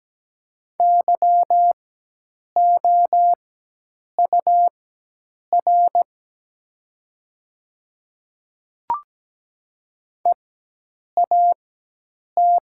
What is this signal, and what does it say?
Morse code, heard as a keyed tone, about 17 wpm, spaced as for 8 wpm, 700 Hz. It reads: YOUR EAT
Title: 4x Character Spacing - Top 300 Words - 17wpm